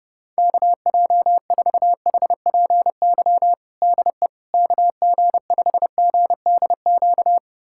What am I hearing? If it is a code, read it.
Morse code, heard as a keyed tone, about 30 wpm, 700 Hz. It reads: KJ4HPY DE KG5GDQ
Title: USA Callsign DE Callsign 30wpm